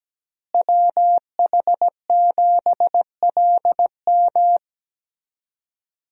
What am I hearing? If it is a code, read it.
Morse code, heard as a keyed tone, about 17 wpm, 700 Hz. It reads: WH7LM
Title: USA Call Signs - 5 Characters - Repeated 3x - 17wpm